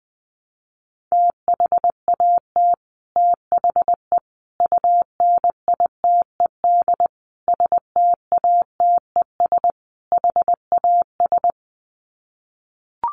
Morse code, heard as a keyed tone, about 20 wpm, 700 Hz. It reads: THAT THE UNITED STATES HAS
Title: Sets of 5 Words Encore 20wpm